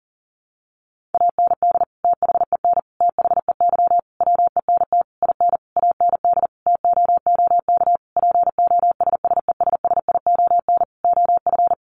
Morse code, heard as a keyed tone, about 40 wpm, 700 Hz. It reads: AND THEN THEY WENT IN AND TOOK POSSESSION OF
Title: Sentences from Top 1400 Words 40wpm